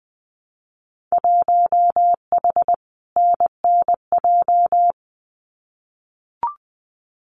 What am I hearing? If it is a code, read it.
Morse code, heard as a keyed tone, about 20 wpm, 700 Hz. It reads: 1H NNJ
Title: Winter Field Day Exchanges 20wpm